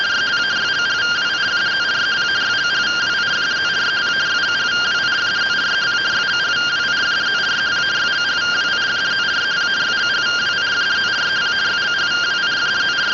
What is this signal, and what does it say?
Two-tone teleprinter signal, about 170 Hz shift.
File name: findings / CT 205 ax mips, iDose 3 0004.gen-03 rtty rx